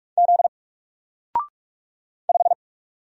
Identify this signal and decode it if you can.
Morse code, heard as a keyed tone, about 46 wpm, 700 Hz. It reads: Z 5
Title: Single Character (Letter:Number:Punctuation:Prosign) Alt 45wpm